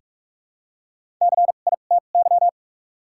Morse code, heard as a keyed tone, about 45 wpm, spaced as for 32 wpm, 700 Hz. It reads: CITY
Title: 2x Character Spacing - Top 300 Words - 45wpm